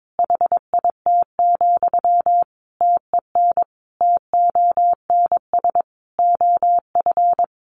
Morse code, heard as a keyed tone, about 22 wpm, 700 Hz. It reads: HIT, TEN TONS OF
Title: A Sound Of Thunder 22wpm